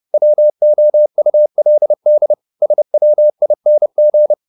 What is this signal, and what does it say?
Morse code, heard as a keyed tone, about 30 wpm, 600 Hz. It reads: WOULD SWING